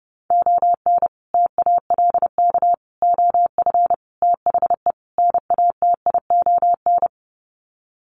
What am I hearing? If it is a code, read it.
Morse code, heard as a keyed tone, about 30 wpm, 700 Hz. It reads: ON TALK OF THE NATION
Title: Sets of 5 Words 30wpm